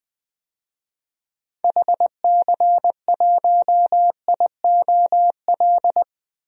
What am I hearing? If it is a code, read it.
Morse code, heard as a keyed tone, about 20 wpm, 700 Hz. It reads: HC1IOL
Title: DXCC-100 Call Signs - 6 Characters - 20wpm